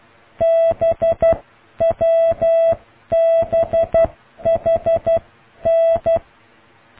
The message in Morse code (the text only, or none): BWBHN